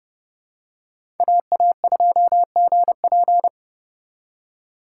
Morse code, heard as a keyed tone, about 30 wpm, 700 Hz. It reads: AA2GP
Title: USA Call Signs - 5 Characters - 3x Character Spacing - 30wpm